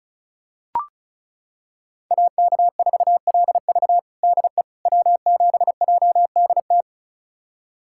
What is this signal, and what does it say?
Morse code, heard as a keyed tone, about 34 wpm, 700 Hz. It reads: AK4LV DE W7JDT
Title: USA Callsign DE Callsign - 5 Characters - 35wpm